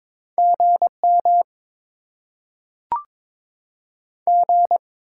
Morse code, heard as a keyed tone, about 22 wpm, 700 Hz. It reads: GM G
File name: Top 20 QSO Elements - 8x Character Spacing - 22wpm